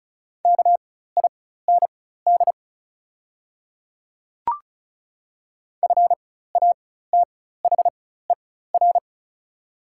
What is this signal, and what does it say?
Morse code, heard as a keyed tone, about 36 wpm, spaced as for 16 wpm, 690 Hz. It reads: KIND FATHER
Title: 4x Character Spacing - Top 200 Words - 35wpm